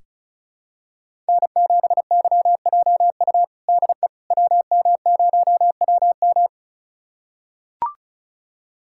Morse code, heard as a keyed tone, about 36 wpm, 690 Hz. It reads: N7YJU DE WM0WM
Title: USA Callsign DE Callsign - 5 Characters - 35wpm